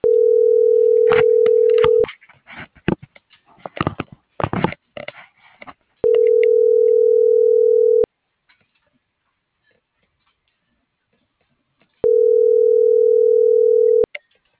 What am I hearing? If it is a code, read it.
Ringback tone (North America), 440 + 480 Hz.
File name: CENTRAL MAERSK / out sandovalmsk - .70656